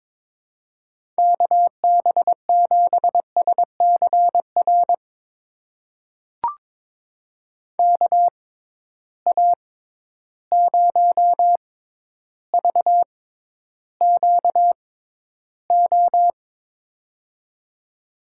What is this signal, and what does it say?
Morse code, heard as a keyed tone, about 22 wpm, spaced as for 11 wpm, 700 Hz. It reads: KB7SCR KA0VQO
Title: USA Call Signs - 6x Character Spacing - 22wpm